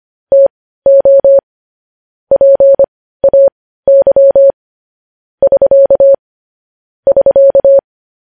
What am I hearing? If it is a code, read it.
Morse code, heard as a keyed tone, about 25 wpm, spaced as for 15 wpm, 550 Hz. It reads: TO PAY <SK> <SK>
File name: podcast / News headlines-1x-25x15